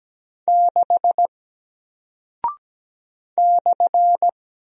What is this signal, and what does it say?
Morse code, heard as a keyed tone, about 17 wpm, 700 Hz. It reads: 6 /
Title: Single Character (Letter:Number:Punctuation:Prosign) 17wpm